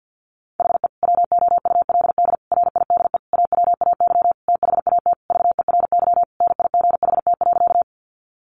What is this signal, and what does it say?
Morse code, heard as a keyed tone, about 50 wpm, 700 Hz. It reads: HE WOULD RIDE AWAY THAT VERY NIGHT.
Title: Sentences from Top 1200 Words 50wpm